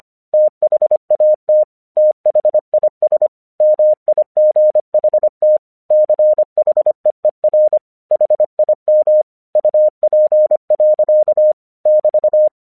Morse code, heard as a keyed tone, about 25 wpm, 610 Hz. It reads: THAT THIS MIGHT CHEER HIM UP. <BT>
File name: 1 English 25wpm / AQC23